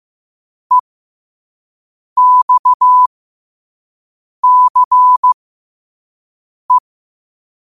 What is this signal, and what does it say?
Morse code, heard as a keyed tone, about 15 wpm, spaced as for 5 wpm, 1000 Hz. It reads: EXCE
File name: qotdfiles / 5wpm